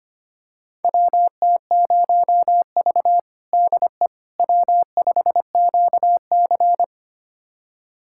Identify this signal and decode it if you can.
Morse code, heard as a keyed tone, about 25 wpm, 700 Hz. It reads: WT0V DE W5QC
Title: USA Callsign DE Callsign - 4 Characters - 25wpm